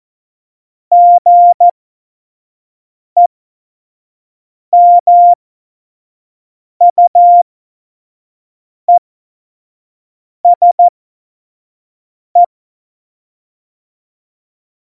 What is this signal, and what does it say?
Morse code, heard as a keyed tone, about 14 wpm, spaced as for 5 wpm, 700 Hz. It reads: GEMUESE
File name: Morsekurs / Witze / Witz 18. Februar 2020, 25 F70